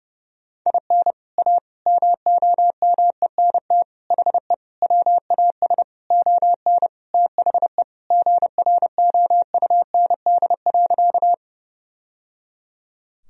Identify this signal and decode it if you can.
Morse code, heard as a keyed tone, about 32 wpm, 700 Hz. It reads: IN A MOMENT HE WAS ON THE GROUND.
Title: Sentences from Top 1000 Words - Review Part 2 - 30wpm